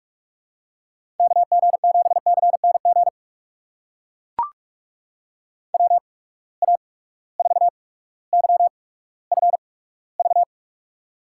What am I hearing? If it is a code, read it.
Morse code, heard as a keyed tone, about 43 wpm, spaced as for 19 wpm, 700 Hz. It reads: KG7CNG WA4YFV
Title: USA Call Signs - 6 Characters - 8x Character Spacing - 45wpm